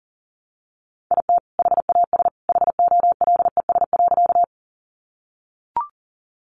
Morse code, heard as a keyed tone, about 40 wpm, 700 Hz. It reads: IT HAS HOLES.
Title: Additional Easy Phrases 40wpm